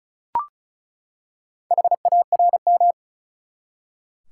Morse code, heard as a keyed tone, about 35 wpm, 700 Hz. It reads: HARM
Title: Four Letter Words - 35wpm